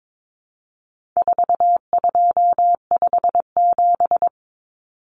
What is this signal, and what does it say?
Morse code, heard as a keyed tone, about 22 wpm, 700 Hz. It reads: 4257
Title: Numbers - 4-Digit 22wpm